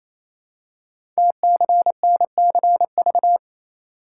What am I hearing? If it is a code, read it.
Morse code, heard as a keyed tone, about 29 wpm, 690 Hz. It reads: TCNCV
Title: Code Groups - 5 Letters - 28wpm